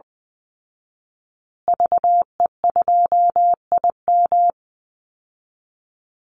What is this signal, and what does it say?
Morse code, heard as a keyed tone, about 20 wpm, 700 Hz. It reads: VE2IM
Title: Call Signs Repeated 2x - 20wpm